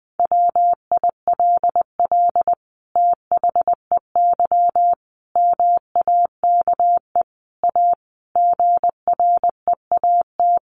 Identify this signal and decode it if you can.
Morse code, heard as a keyed tone, about 19 wpm, 700 Hz. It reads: WILL THEY MAKE A GREAT
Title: Sentences from Top 1500 Words 20wpm